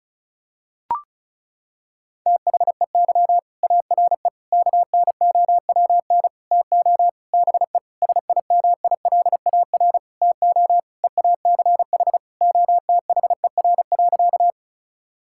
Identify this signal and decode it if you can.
Morse code, heard as a keyed tone, about 34 wpm, 700 Hz. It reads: THEY ARE KNOWN TO BE SIMILAR TO EACH OTHER.
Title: Sentences from Top 1000 Words - Review Part 5 - 35wpm